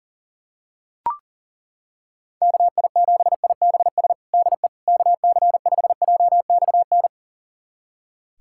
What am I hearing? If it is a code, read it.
Morse code, heard as a keyed tone, about 40 wpm, 700 Hz. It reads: KI7IBS DE KC5JXN
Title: USA Callsign DE Callsign 40wpm